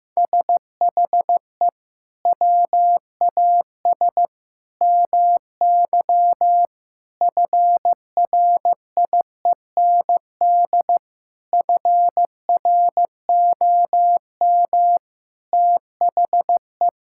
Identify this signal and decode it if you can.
Morse code, heard as a keyed tone, about 14 wpm, 700 Hz. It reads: SHE WAS MY FRIEND FROM THE
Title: Sentences from Top 1000 Words - Review Part 5 - 15wpm